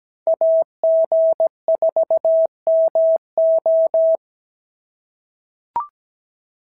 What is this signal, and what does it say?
Morse code, heard as a keyed tone, about 16 wpm, 650 Hz. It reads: AG4MO
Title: SOTA QSO Practice 17wpm